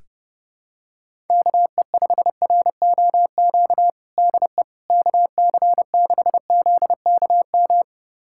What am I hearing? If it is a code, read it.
Morse code, heard as a keyed tone, about 30 wpm, 700 Hz. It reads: KE5ROQ DE KC6ZKM
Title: USA Callsign DE Callsign - 6 Characters - 30wpm